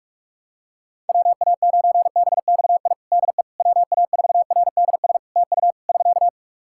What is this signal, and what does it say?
Morse code, heard as a keyed tone, about 46 wpm, 700 Hz. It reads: WA9BXI DE WA4RDS TU <SK>
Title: Practice QSO - Exchange 4 - Ending - 45wpm